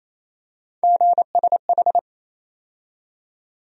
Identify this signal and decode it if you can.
Morse code, heard as a keyed tone, about 28 wpm, 700 Hz. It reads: GSH
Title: USA Call Sign Postfixes - 3 Letters 28wpm